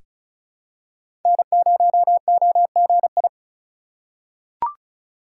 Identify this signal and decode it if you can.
Morse code, heard as a keyed tone, about 34 wpm, 700 Hz. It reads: N0OGI
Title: USA Call Signs - 5 Characters - 8x Character Spacing - 35wpm